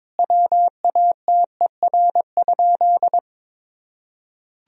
Morse code, heard as a keyed tone, about 23 wpm, 700 Hz. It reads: WATER?
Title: Sentences from Top 1000 Words - Review Part 3 - 22wpm